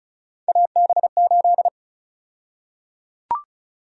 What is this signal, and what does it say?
Morse code, heard as a keyed tone, about 36 wpm, 700 Hz. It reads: AB8
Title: USA Call Sign Prefixes - Repeated 2x - 35wpm